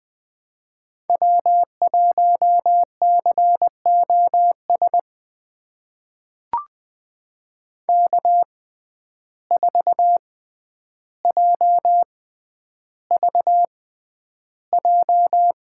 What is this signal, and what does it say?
Morse code, heard as a keyed tone, about 20 wpm, spaced as for 10 wpm, 700 Hz. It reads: W1COS K4JVJ
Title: USA Call Signs - 5 Characters - 6x Character Spacing - 20wpm